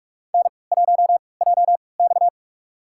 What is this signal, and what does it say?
Morse code, heard as a keyed tone, about 46 wpm, spaced as for 25 wpm, 700 Hz. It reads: N1JX
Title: USA Call Signs - 4 Characters - 3x Character Spacing - 45wpm